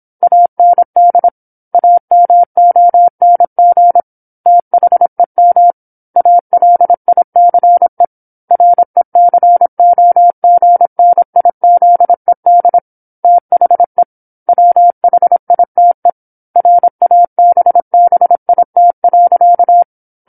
Morse code, heard as a keyed tone, about 27 wpm, 700 Hz. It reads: AND AMONG THEM ALICE RECOGNIZED THE WHITE RABBIT.
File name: 08-Chapter-08 26 WPM